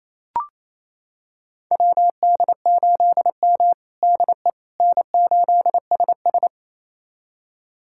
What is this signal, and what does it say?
Morse code, heard as a keyed tone, about 28 wpm, 700 Hz. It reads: WD8M DE N8SS
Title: USA Callsign DE Callsign - 4 Characters - 28wpm